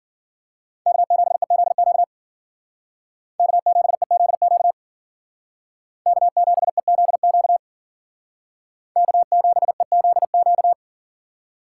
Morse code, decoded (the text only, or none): K7EZQ K7EZQ K7EZQ K7EZQ